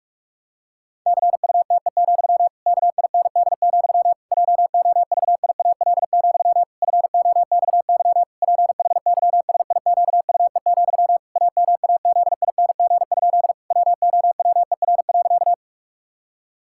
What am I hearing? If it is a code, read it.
Morse code, heard as a keyed tone, about 46 wpm, 700 Hz. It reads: CUTE, KIND, JOVIAL, FOXY PHYSIQUE, AMAZING? WOWER.
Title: Pangram Challenge v2 - 45wpm